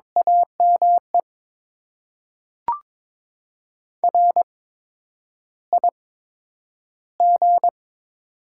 Morse code, heard as a keyed tone, about 22 wpm, spaced as for 9 wpm, 700 Hz. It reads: AME RIG
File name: Top 20 QSO Elements - 8x Character Spacing - 22wpm